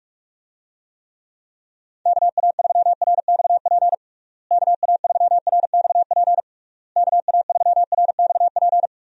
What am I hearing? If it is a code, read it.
Morse code, heard as a keyed tone, about 41 wpm, 700 Hz. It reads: KA3RXP KA3RXP KA3RXP
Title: USA Call Signs - 6 Characters - Repeated 3x - 45wpm